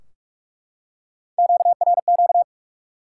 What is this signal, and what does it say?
Morse code, heard as a keyed tone, about 44 wpm, 700 Hz. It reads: QRQ
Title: Top 90 QSO Elements 45wpm